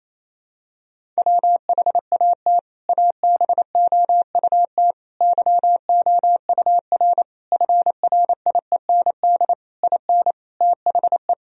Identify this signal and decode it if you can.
Morse code, heard as a keyed tone, about 28 wpm, 700 Hz. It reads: WHAT ABOUT YOUR FRIEND IN THE
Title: Sentences from Top 1000 Words - Review Part 1 - 28wpm